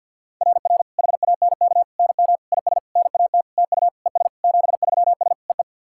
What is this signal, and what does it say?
Morse code, heard as a keyed tone, about 49 wpm, 690 Hz. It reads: RR HANK NM IS NAT TU ES 73S EE
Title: SOTA QSO Practice 50wpm